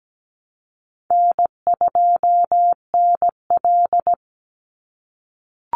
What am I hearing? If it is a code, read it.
Morse code, heard as a keyed tone, about 17 wpm, 700 Hz. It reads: N2NL
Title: Easy Call Signs - 17wpm